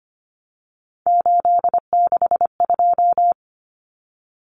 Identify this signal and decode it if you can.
Morse code, heard as a keyed tone, about 25 wpm, 700 Hz. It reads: 862
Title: Numbers - 3-Digit 25wpm